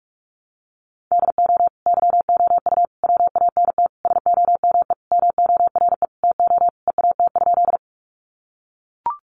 Morse code, heard as a keyed tone, about 44 wpm, 700 Hz. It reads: DO YOU WANT SOME MORE TO EAT?
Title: Sentences from Top 300 Words 45wpm